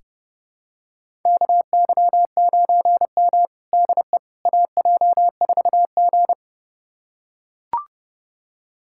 Morse code, heard as a keyed tone, about 30 wpm, 700 Hz. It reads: KY9M DE AJ4G
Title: USA Callsign DE Callsign - 4 Characters - 30wpm